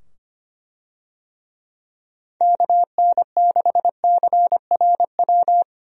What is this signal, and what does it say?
Morse code, heard as a keyed tone, about 25 wpm, 700 Hz. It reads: KN6CRW
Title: USA Call Signs - 6 Characters - 25wpm